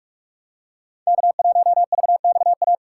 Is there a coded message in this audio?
Yes: Morse code from 1.1 to 2.7 s, heard as a keyed tone, about 43 wpm, 700 Hz. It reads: K1VXA